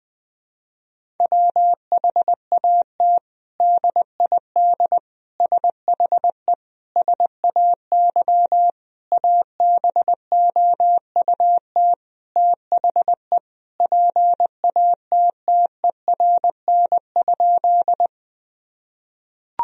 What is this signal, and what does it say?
Morse code, heard as a keyed tone, about 21 wpm, 700 Hz. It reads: WHAT DID SHE SAY ABOUT THE PATTERN?
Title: Sentences from Top 1000 Words - Review Part 2 - 20wpm